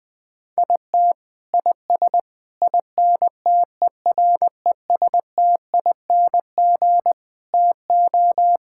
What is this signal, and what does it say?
Morse code, heard as a keyed tone, about 20 wpm, 700 Hz. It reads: IT IS INTERESTING TO
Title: Sets of 6 Words - 20wpm